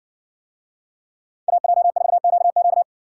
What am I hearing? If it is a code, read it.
Morse code, heard as a keyed tone, about 57 wpm, 700 Hz. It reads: AJ4QQ